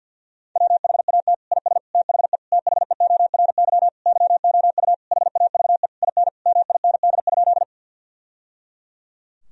Morse code, heard as a keyed tone, about 48 wpm, 700 Hz. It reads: WHAT IS THE THEORY YOU HAVE IN MIND?
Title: Sentences from Top 1000 Words - Review Part 3 - 50wpm